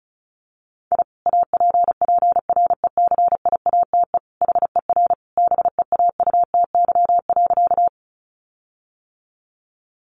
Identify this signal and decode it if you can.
Morse code, heard as a keyed tone, about 34 wpm, 700 Hz. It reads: I APPRECIATE HER BEAUTY.
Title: Sentences from Top 1300 Words 35wpm